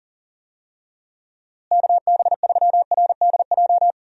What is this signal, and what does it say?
Morse code, heard as a keyed tone, about 38 wpm, 700 Hz. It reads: KB3RDJ
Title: USA Call Signs - 6 Characters - Repeated 2x - 40wpm